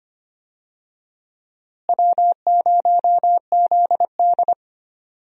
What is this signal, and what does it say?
Morse code, heard as a keyed tone, about 25 wpm, 700 Hz. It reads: W0ZD
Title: USA Call Signs - 4 Characters - Repeated 2x - 25wpm